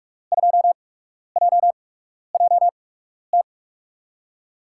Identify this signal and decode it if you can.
Morse code, heard as a keyed tone, about 43 wpm, spaced as for 12 wpm, 700 Hz. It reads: 2JJT
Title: USA Call Signs - 6 Characters - 8x Character Spacing - 45wpm